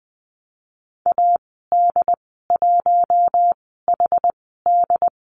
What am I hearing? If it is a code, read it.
Morse code, heard as a keyed tone, about 20 wpm, spaced as for 15 wpm, 700 Hz. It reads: AD1HD
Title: 128 Lesson 21g - Number 6 - Call Signs 20wpm